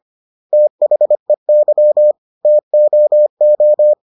Morse code, heard as a keyed tone, about 25 wpm, 600 Hz. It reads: THEY TOO